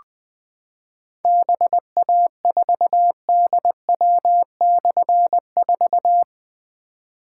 Morse code, heard as a keyed tone, about 21 wpm, 700 Hz. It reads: BA4DW/4
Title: Difficult Call Signs 20wpm